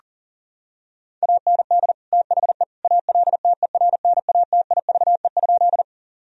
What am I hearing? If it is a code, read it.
Morse code, heard as a keyed tone, about 40 wpm, 700 Hz. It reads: AND THE ALTERNATIVE?
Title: Sentences from Top 1000 Words - Review Part 4 - 40wpm